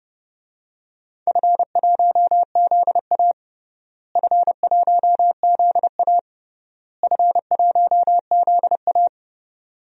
Morse code, heard as a keyed tone, about 30 wpm, 700 Hz. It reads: F1ZA F1ZA F1ZA
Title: DXCC-100 Call Signs - 4 Characters - Repeated 3x - 30wpm